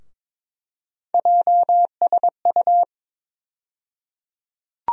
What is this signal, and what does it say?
Morse code, heard as a keyed tone, about 21 wpm, 700 Hz. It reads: JSU
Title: USA Call Sign Postfixes - 3 Letters 22wpm